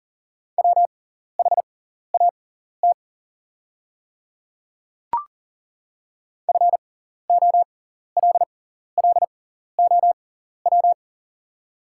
Morse code, heard as a keyed tone, about 39 wpm, spaced as for 14 wpm, 700 Hz. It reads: WHAT FOLLOW
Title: 6x Character Spacing - Top 200 Words - 40wpm